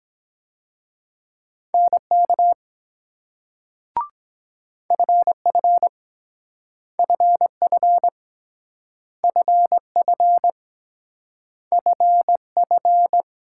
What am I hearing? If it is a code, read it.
Morse code, heard as a keyed tone, about 23 wpm, 700 Hz. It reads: NK FF FF FF FF